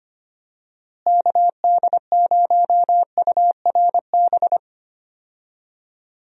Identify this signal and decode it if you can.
Morse code, heard as a keyed tone, about 25 wpm, 700 Hz. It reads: KD0URB
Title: USA Call Signs - 6 Characters - Repeated 3x - 25wpm